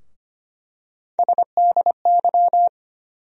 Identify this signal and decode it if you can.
Morse code, heard as a keyed tone, about 25 wpm, 700 Hz. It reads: SDY
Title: USA Call Sign Postfixes - 3 Letters 25wpm